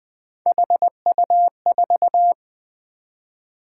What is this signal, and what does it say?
Morse code, heard as a keyed tone, about 20 wpm, 700 Hz. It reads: HU4
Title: Three Letter-Number Combinations - 20wpm